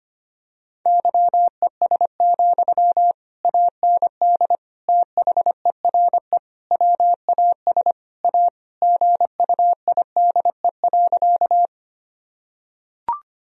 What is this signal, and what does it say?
Morse code, heard as a keyed tone, about 25 wpm, 700 Hz. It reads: YES, AND THERE WAS A GUIDE.
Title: Sentences from Top 1000 Words - Review Part 3 - 25wpm